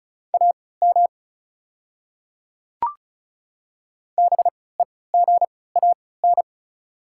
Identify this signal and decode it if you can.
Morse code, heard as a keyed tone, about 34 wpm, spaced as for 20 wpm, 700 Hz. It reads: AM BEGAN
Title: 3x Character Spacing - Top 500 Words - 35wpm